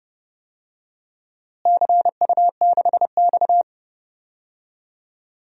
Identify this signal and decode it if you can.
Morse code, heard as a keyed tone, about 30 wpm, 700 Hz. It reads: CU6X
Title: DXCC-100 Call Signs - 4 Characters - Repeated 2x - 30wpm